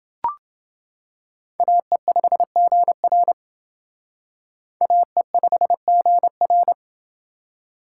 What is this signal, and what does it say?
Morse code, heard as a keyed tone, about 28 wpm, 700 Hz. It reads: AE5GR AE5GR